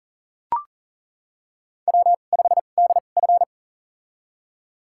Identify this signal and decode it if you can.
Morse code, heard as a keyed tone, about 38 wpm, spaced as for 28 wpm, 690 Hz. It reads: W5DF
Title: USA Call Signs - 4 Characters - 2x Character Spacing - 40wpm